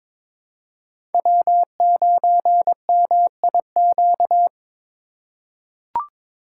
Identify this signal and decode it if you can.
Morse code, heard as a keyed tone, about 22 wpm, 690 Hz. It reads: W9MIQ
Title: USA Call Signs - 5 Characters - 2x Character Spacing - 22wpm